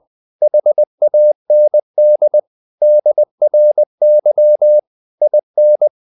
Morse code, heard as a keyed tone, about 20 wpm, 600 Hz. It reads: HAND DRY IN
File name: the old man and the sea part 2 20wpm